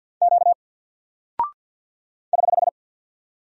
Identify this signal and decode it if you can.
Morse code, heard as a keyed tone, about 48 wpm, 700 Hz. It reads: Q <HH>
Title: Single Character (Letter:Number:Punctuation:Prosign) Alt 50wpm